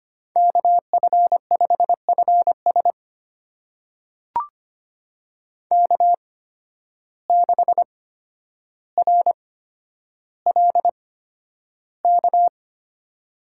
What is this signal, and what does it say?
Morse code, heard as a keyed tone, about 25 wpm, spaced as for 11 wpm, 700 Hz. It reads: KF5FS K6RLK
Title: USA Call Signs - 5 Characters - 8x Character Spacing - 25wpm